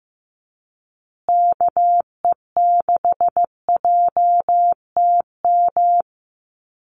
Morse code, heard as a keyed tone, about 15 wpm, 700 Hz. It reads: KE6JTM
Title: USA Call Signs Repeated 3x 15wpm